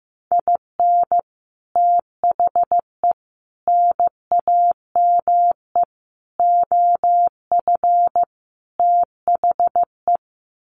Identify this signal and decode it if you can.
Morse code, heard as a keyed tone, about 15 wpm, 700 Hz. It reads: IN THE NAME OF THE
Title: Sets of 5 Words Encore 15wpm